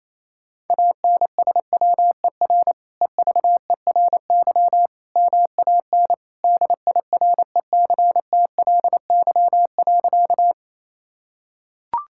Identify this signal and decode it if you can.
Morse code, heard as a keyed tone, about 28 wpm, 700 Hz. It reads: ANSWER EVERY MAN DIRECTLY.
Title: Sentences from Top 1000 Words - Review Part 5 - 28wpm